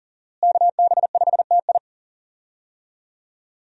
Morse code, heard as a keyed tone, about 37 wpm, 700 Hz. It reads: KB5TI
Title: USA Call Signs - 5 Characters - Repeated 2x - 40wpm